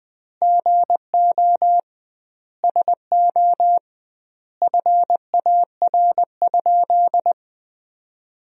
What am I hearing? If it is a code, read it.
Morse code, heard as a keyed tone, about 20 wpm, 700 Hz. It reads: GO SO FAR?
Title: 2x Word-Spacing - Sentences from Top 300 Words - 20wpm